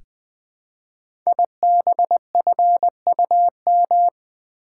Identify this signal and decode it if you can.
Morse code, heard as a keyed tone, about 20 wpm, 700 Hz. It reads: IBFUM
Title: Code Groups - 5 Letter:Numbers - 20wpm